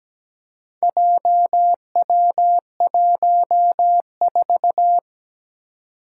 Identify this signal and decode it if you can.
Morse code, heard as a keyed tone, about 17 wpm, 700 Hz. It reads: JW14
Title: Four Challenging Letter-Number Combinations 17wpm